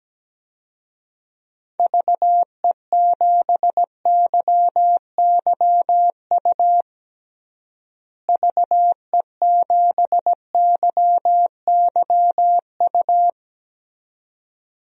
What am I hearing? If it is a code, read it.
Morse code, heard as a keyed tone, about 17 wpm, 700 Hz. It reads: VE7YYU VE7YYU
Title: DXCC-100 Call Signs - 6 Characters - Repeated 2x - 17wpm